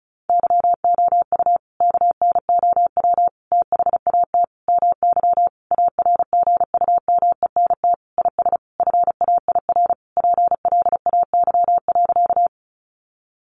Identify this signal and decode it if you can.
Morse code, heard as a keyed tone, about 34 wpm, 700 Hz. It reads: YOU KNOW THAT MY ARGUMENT IS FAIR PLAY.
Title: Sentences from Top 1400 Words 35wpm